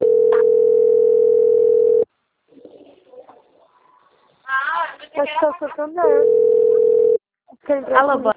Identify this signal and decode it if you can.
Ringback tone (North America), 440 + 480 Hz, with speech.
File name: out- - - - - .13380